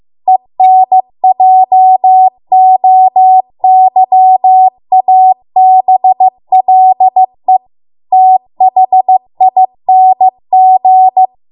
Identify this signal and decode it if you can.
Morse code, heard as a keyed tone, about 15 wpm, 750 Hz. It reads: ENJOYABLE THING